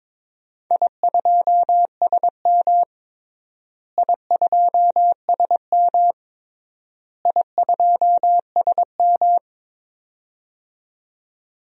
Morse code, heard as a keyed tone, about 21 wpm, 700 Hz. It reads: I2SM I2SM I2SM
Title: DXCC-100 Call Signs - 4 Characters - Repeated 3x - 22wpm